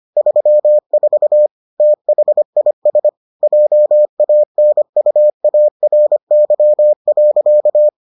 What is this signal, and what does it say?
Morse code, heard as a keyed tone, about 25 wpm, 600 Hz. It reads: 34 THIS JANUARY.